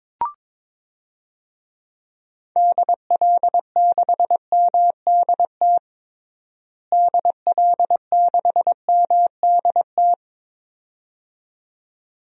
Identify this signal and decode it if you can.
Morse code, heard as a keyed tone, about 22 wpm, 700 Hz. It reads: DL6MDT DL6MDT
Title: DXCC-100 Call Signs - 6 Characters - Repeated 2x - 22wpm